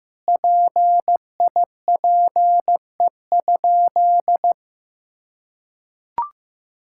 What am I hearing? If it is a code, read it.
Morse code, heard as a keyed tone, about 15 wpm, 700 Hz. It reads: PIPE?
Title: Sentences from Top 1600 Words 15wpm